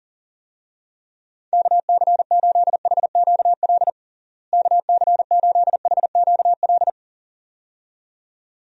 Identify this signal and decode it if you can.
Morse code, heard as a keyed tone, about 40 wpm, 700 Hz. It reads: KC8HQL KC8HQL
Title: USA Call Signs - 6 Characters - Repeated 2x - 40wpm